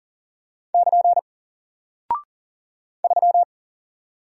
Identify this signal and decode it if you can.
Morse code, heard as a keyed tone, about 40 wpm, 700 Hz. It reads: <KN> 3
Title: Single Character (Letter:Number:Punctuation:Prosign) Alt 40wpm